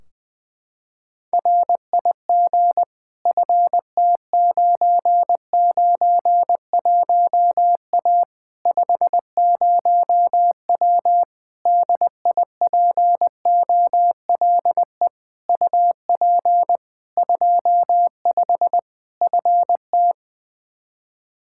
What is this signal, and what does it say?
Morse code, heard as a keyed tone, about 20 wpm, 700 Hz. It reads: RIG FT991A 50W DIPOLE UP 25 FT
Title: Practice QSO - Exchange 2 - Info - 20wpm